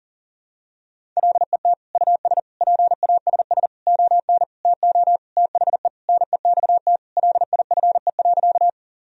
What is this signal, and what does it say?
Morse code, heard as a keyed tone, about 40 wpm, 700 Hz. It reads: LET US PASS ON TO THE NEXT LIFE.
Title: Sentences from Top 1000 Words - Review Part 1 - 40wpm